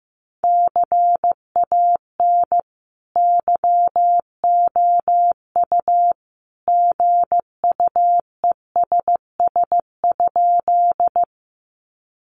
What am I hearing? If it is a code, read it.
Morse code, heard as a keyed tone, about 15 wpm, 700 Hz. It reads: CAN YOU GUESS?
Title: Sentences from Top 1500 Words 15wpm